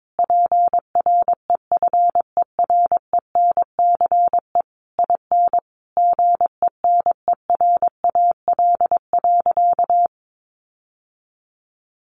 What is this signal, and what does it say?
Morse code, heard as a keyed tone, about 22 wpm, 700 Hz. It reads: PREFERENCE IN GENERAL.
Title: Sentences from Top 1500 Words 22wpm